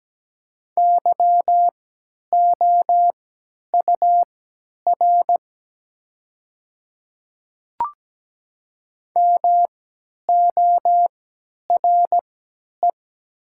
Morse code, heard as a keyed tone, about 17 wpm, spaced as for 10 wpm, 700 Hz. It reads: YOUR MORE
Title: 3x Character Spacing - Top 200 Words - 17wpm